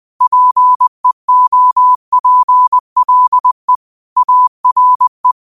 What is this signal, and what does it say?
Morse code, heard as a keyed tone, about 20 wpm, 1000 Hz. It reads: PEOPLE ARE
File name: qotdfiles / 20wpm